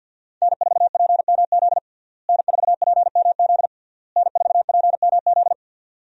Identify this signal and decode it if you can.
Morse code, heard as a keyed tone, about 52 wpm, 700 Hz. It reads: N4PMZ N4PMZ N4PMZ
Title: USA Call Signs - 5 Characters - Repeated 3x - 50wpm